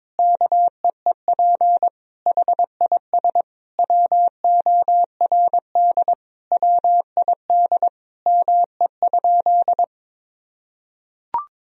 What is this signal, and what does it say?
Morse code, heard as a keyed tone, about 22 wpm, 700 Hz. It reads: KEEP HIS WORD WID ME?
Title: Sentences from Top 300 Words and Abbreviations 22wpm